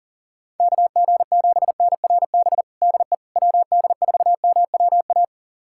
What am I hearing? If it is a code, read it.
Morse code, heard as a keyed tone, about 40 wpm, 700 Hz. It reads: KG7NRB DE WD4MWA
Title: USA Callsign DE Callsign 40wpm